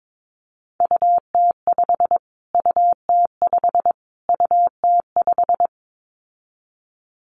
Morse code, heard as a keyed tone, about 22 wpm, 700 Hz. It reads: UT5 UT5 UT5
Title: Callsign Prefixes for Top 25 DXCC - Repeated 3x - 22wpm